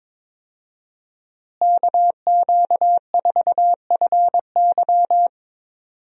Morse code, heard as a keyed tone, about 22 wpm, 700 Hz. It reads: KQ4FY
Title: USA Call Signs - 5 Characters - Repeated 2x - 22wpm